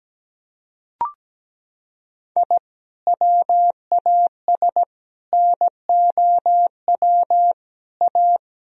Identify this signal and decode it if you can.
Morse code, heard as a keyed tone, about 17 wpm, 700 Hz. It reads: I WAS NOW A
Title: Sentences from Top 1000 Words - Review Part 2 - 17wpm